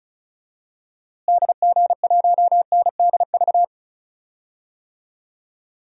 Morse code, heard as a keyed tone, about 36 wpm, 700 Hz. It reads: DG1NDV
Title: DXCC-100 Call Signs - 6 Characters - Repeated 3x - 35wpm